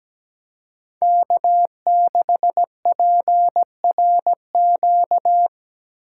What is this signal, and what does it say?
Morse code, heard as a keyed tone, about 17 wpm, 700 Hz. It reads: K6PRQ
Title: USA Call Signs - 5 Characters - 3x Character Spacing - 17wpm